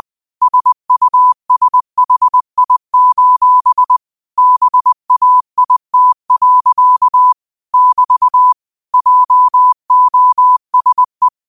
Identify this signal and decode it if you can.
Morse code, heard as a keyed tone, about 20 wpm, 1000 Hz. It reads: SUSHI: BAIT. <BT> JOSE